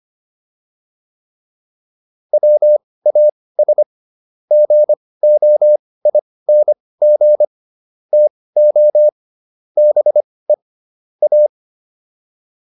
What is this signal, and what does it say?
Morse code, heard as a keyed tone, about 25 wpm, spaced as for 18 wpm, 600 Hz. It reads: WAS GOING TO BE A